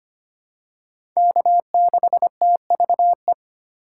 Morse code, heard as a keyed tone, about 25 wpm, 700 Hz. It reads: K6TVE
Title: USA Call Signs - 5 Characters - 4x Character Spacing - 25wpm